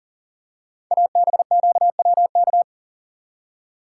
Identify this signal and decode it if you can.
Morse code, heard as a keyed tone, about 40 wpm, 690 Hz. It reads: ABQWK
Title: Code Groups - 5 Letters - 40wpm